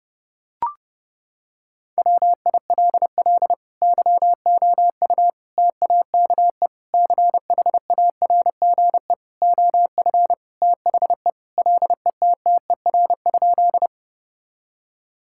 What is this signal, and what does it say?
Morse code, heard as a keyed tone, about 30 wpm, 700 Hz. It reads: WILL YOU TAKE CHARGE OF THE LETTER?
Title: Sentences from Top 1000 Words - Review Part 6 - 30wpm